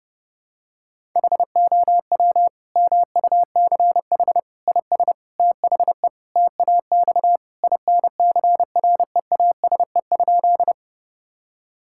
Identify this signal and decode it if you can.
Morse code, heard as a keyed tone, about 30 wpm, 700 Hz. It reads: HOW MUCH IS THE TAX INCREASE?
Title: Sentences from Top 1000 Words - Review Part 5 - 30wpm